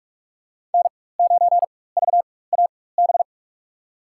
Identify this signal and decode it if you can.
Morse code, heard as a keyed tone, about 41 wpm, spaced as for 20 wpm, 700 Hz. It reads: N9VAB
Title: USA Call Signs - 4x Character Spacing - 45wpm